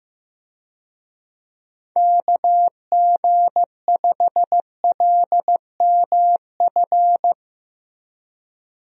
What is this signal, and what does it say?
Morse code, heard as a keyed tone, about 15 wpm, 700 Hz. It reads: KG5LMF
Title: USA Call Signs - 6 Characters - Repeated 2x - 15wpm